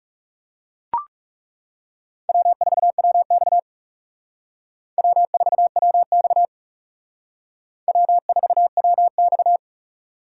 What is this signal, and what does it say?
Morse code, heard as a keyed tone, about 37 wpm, 700 Hz. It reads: W4WX W4WX W4WX
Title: Speed Racing - USA 1x2 Call Signs 30wpm